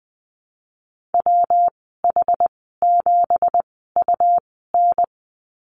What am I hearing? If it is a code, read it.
Morse code, heard as a keyed tone, about 20 wpm, spaced as for 14 wpm, 700 Hz. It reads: WH7UN
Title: 213 Lesson 34g - Number 8 - Call Signs 20wpm